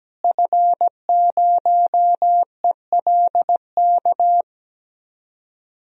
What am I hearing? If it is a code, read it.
Morse code, heard as a keyed tone, about 17 wpm, 700 Hz. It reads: F0ELK
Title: DXCC-100 Call Signs - 5 Characters - Repeated 2x - 17wpm